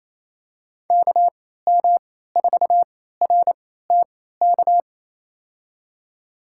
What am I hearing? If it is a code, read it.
Morse code, heard as a keyed tone, about 27 wpm, spaced as for 16 wpm, 700 Hz. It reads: KM4RTK
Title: USA Call Signs - 3x Character Spacing - 28wpm